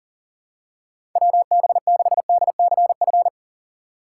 Morse code, heard as a keyed tone, about 40 wpm, 700 Hz. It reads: WB6DCF
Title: USA Call Signs - 6 Characters - 6x Character Spacing - 40wpm